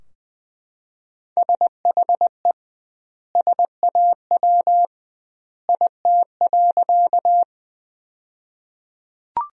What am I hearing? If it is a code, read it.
Morse code, heard as a keyed tone, about 21 wpm, 700 Hz. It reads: SHE SAW IT.
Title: 2x Word-Spacing - Sentences from Top 300 Words - 20wpm